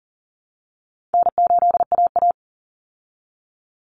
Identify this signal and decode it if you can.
Morse code, heard as a keyed tone, about 40 wpm, 700 Hz. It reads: N8AA
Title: Call Signs Repeated 2x - 40wpm